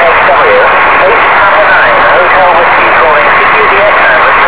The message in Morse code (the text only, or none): none